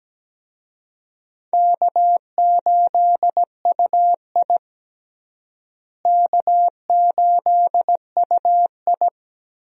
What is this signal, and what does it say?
Morse code, heard as a keyed tone, about 17 wpm, 700 Hz. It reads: K8UI K8UI
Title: USA Call Signs - 4 Characters - Repeated 3x - 17wpm